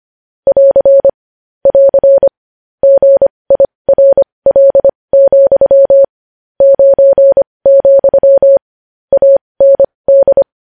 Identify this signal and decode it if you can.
Morse code, heard as a keyed tone, about 25 wpm, spaced as for 20 wpm, 550 Hz. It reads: <AR> <AR> GIRL, 9, AND
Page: news-headlines-in-morse-code-at-20-wpm